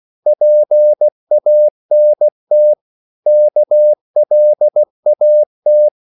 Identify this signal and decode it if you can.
Morse code, heard as a keyed tone, about 16 wpm, 600 Hz. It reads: PANT KLAT
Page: Korte danske ord 16wpm